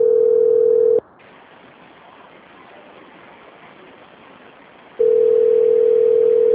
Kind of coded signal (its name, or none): telephone call progress tone